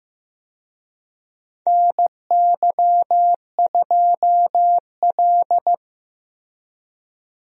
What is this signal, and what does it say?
Morse code, heard as a keyed tone, about 15 wpm, 700 Hz. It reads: NY2L
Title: USA Call Signs - 4 Characters - Repeated 3x - 15wpm